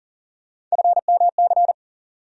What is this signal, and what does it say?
Morse code, heard as a keyed tone, about 40 wpm, 700 Hz. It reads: FMC
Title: USA Call Sign Postfixes - 3 Letters 40wpm